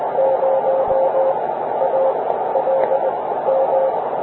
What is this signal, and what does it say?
Morse code, heard as a keyed tone, about 20 wpm, spaced as for 13 wpm, 530 Hz, with radio noise. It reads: 0ALM